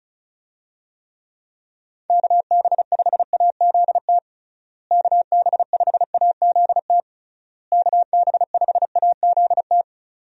Morse code, heard as a keyed tone, about 34 wpm, 700 Hz. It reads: KB5AZT KB5AZT KB5AZT
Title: USA Call Signs - 6 Characters - Repeated 3x - 35wpm